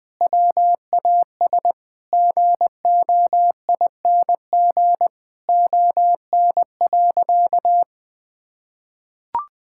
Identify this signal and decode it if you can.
Morse code, heard as a keyed tone, about 20 wpm, 700 Hz. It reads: WAS GOING ON.
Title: Sentences with 7-Word Phrases - 20wpm